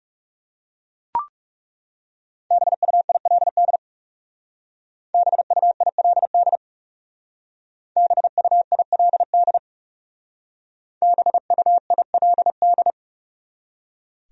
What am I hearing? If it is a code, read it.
Morse code, heard as a keyed tone, about 38 wpm, 700 Hz. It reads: BUILD BUILD BUILD BUILD